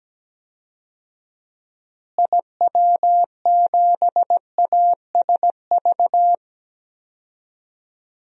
Morse code, heard as a keyed tone, about 17 wpm, 700 Hz. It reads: IW7ASV
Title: DXCC-100 Call Signs - 6 Characters - Repeated 2x - 17wpm